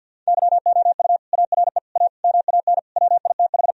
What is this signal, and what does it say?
Morse code, heard as a keyed tone, about 52 wpm, 700 Hz. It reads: YOU ARE A MAN WITH
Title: Sentences from Top 1000 Words - Review Part 3 - 50wpm